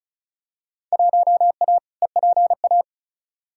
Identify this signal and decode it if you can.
Morse code, heard as a keyed tone, about 34 wpm, 700 Hz. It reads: 1A EPA
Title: Field Day Exchanges - 35wpm